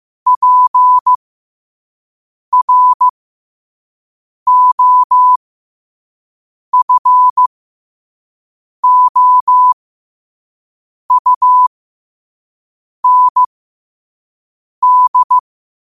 Morse code, heard as a keyed tone, about 15 wpm, spaced as for 5 wpm, 1000 Hz. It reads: PROFOUND
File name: qotdfiles / 5wpm